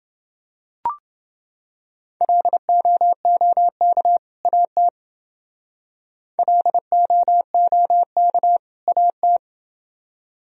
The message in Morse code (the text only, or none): LOOK AT LOOK AT